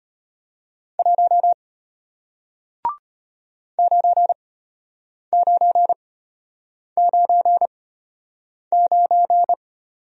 Morse code, heard as a keyed tone, about 34 wpm, spaced as for 8 wpm, 700 Hz. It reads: 1 9999